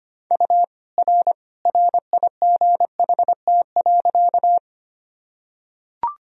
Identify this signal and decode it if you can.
Morse code, heard as a keyed tone, about 25 wpm, 700 Hz. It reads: U R RIGHT.
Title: Sentences from Top 200 Words and Abbreviations 25wpm